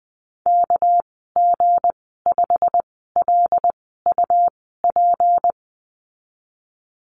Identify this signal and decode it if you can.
Morse code, heard as a keyed tone, about 20 wpm, spaced as for 15 wpm, 700 Hz. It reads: KG5LUP
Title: 219 Lesson 35g - Number 0 - Call Signs 20wpm